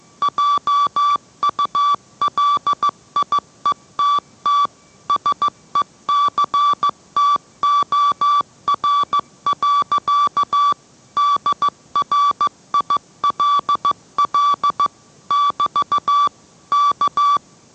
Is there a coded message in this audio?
Yes: Morse code from 0.2 to 17.4 s, heard as a keyed tone, about 15 wpm, 1160 Hz. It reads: JULIETT SECTOR. DRILL <BT> K